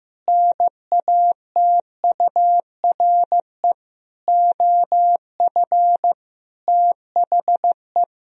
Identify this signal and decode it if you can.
Morse code, heard as a keyed tone, about 15 wpm, 700 Hz. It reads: NATURE OF THE
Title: Sentences from Top 1000 Words - Review Part 4 - 15wpm